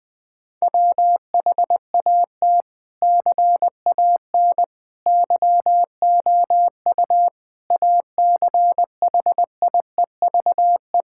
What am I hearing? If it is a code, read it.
Morse code, heard as a keyed tone, about 20 wpm, 700 Hz. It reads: WHAT CAN YOU ACHIEVE
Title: Sentences from Top 1000 Words - Review Part 6 - 20wpm